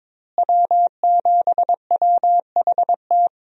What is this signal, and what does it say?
Morse code, heard as a keyed tone, about 22 wpm, 700 Hz. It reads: W7WHT
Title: USA Call Signs - 5 Characters - Repeated 3x - 22wpm